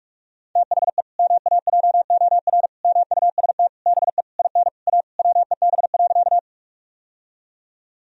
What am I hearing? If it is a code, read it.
Morse code, heard as a keyed tone, about 46 wpm, 700 Hz. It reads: THE MAJOR MUST BE IN A WEB.
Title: Sentences from Top 1000 Words - Review Part 4 - 45wpm